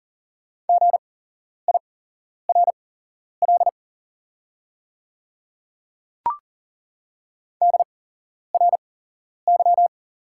Morse code, heard as a keyed tone, about 39 wpm, spaced as for 11 wpm, 690 Hz. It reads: GIRL DRY
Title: 8x Character Spacing - Top 500 Words - 40wpm